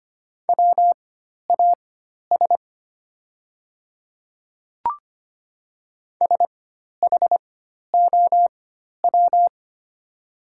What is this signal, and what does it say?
Morse code, heard as a keyed tone, about 25 wpm, spaced as for 12 wpm, 700 Hz. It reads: WAS SHOW
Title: 4x Character Spacing - Top 200 Words - 25wpm